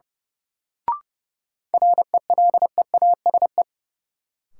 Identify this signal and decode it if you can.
Morse code, heard as a keyed tone, about 30 wpm, 700 Hz. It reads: RELEASE
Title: Sentences from Top 1100 Words - New Word Drill Alt - 30wpm